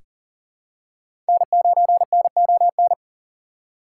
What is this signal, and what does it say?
Morse code, heard as a keyed tone, about 40 wpm, 700 Hz. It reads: N9NON
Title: USA Call Signs - 3x Character Spacing - 40wpm